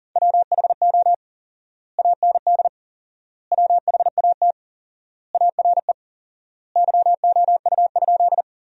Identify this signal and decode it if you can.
Morse code, heard as a keyed tone, about 40 wpm, 700 Hz. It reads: WHO AND WHAT ARE YOU?
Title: 4x Word-Spacing - Sentences from Top 100 Words - 40wpm